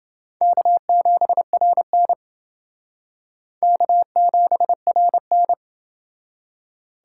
Morse code, heard as a keyed tone, about 28 wpm, 700 Hz. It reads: K7RN K7RN